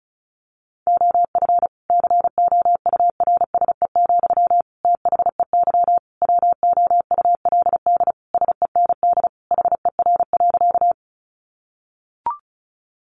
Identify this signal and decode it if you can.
Morse code, heard as a keyed tone, about 34 wpm, 700 Hz. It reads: OF COURSE, THEY WOULD SEND HER.
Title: Sentences from Top 1500 Words 35wpm